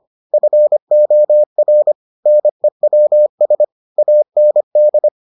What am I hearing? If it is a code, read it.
Morse code, heard as a keyed tone, about 25 wpm, 600 Hz. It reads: FOR NEWS AND